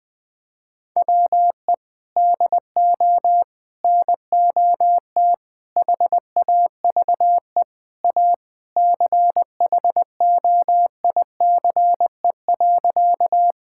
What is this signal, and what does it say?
Morse code, heard as a keyed tone, about 20 wpm, 700 Hz. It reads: WE DO NOT HAVE A CHOICE.
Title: Sentences with 3-Word Phrases - 20wpm